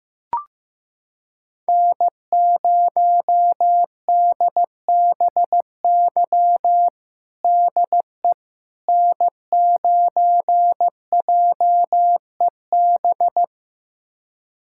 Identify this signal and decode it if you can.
Morse code, heard as a keyed tone, about 16 wpm, 700 Hz. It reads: N0DBY DE N9JEB
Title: USA Callsign DE Callsign 15wpm